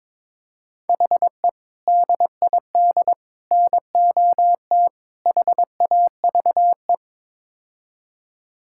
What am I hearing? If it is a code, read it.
Morse code, heard as a keyed tone, about 22 wpm, 700 Hz. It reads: HE DID NOT HAVE
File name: Build A Phrase - 5-Word Phrases - 22wpm